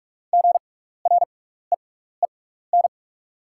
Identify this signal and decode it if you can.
Morse code, heard as a keyed tone, about 44 wpm, spaced as for 15 wpm, 700 Hz. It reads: GREEN
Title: 6x Character Spacing - Top 500 Words - 45wpm